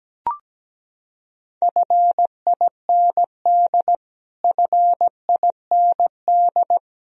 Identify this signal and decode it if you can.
Morse code, heard as a keyed tone, about 17 wpm, 700 Hz. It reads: FIND FIND
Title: Top 100 Words Repeated 17wpm